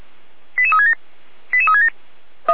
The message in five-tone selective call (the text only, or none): 88367 88367